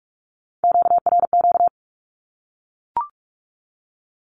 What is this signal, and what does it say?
Morse code, heard as a keyed tone, about 44 wpm, 700 Hz. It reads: QRQ
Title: Q-Codes Repeated 3x 45wpm